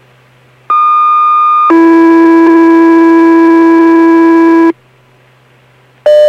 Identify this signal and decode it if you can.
Two-tone page, heard as a one-second tone, then a three-second tone, about 1220 then 330 Hz. It is heard from 0.7 to 4.7 s.